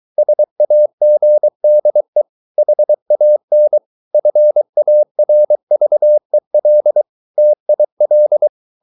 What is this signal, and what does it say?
Morse code, heard as a keyed tone, about 23 wpm, 600 Hz. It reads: SAGDE HAN FARVEL TIL
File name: Ferie QRS 23wpm